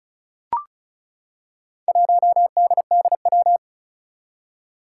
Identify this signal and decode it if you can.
Morse code, heard as a keyed tone, about 36 wpm, 700 Hz. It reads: 1DDW
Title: Four Challenging Letter-Number Combinations 35wpm